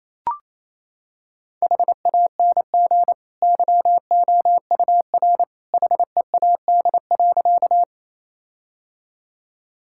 Morse code, heard as a keyed tone, about 28 wpm, 700 Hz. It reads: HANG YOUR HEAD.
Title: Sentences from Top 1600 Words 28wpm